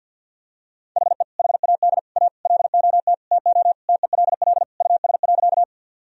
Morse code, heard as a keyed tone, about 46 wpm, 700 Hz. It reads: HE HAD A LOT TO TELL US.
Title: Sentences from Top 1000 Words - Review Part 3 - 50wpm